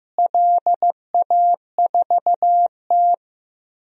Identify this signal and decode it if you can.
Morse code, heard as a keyed tone, about 15 wpm, 700 Hz. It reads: LA4T
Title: DXCC-100 Call Signs - 4 Characters - Repeated 3x - 15wpm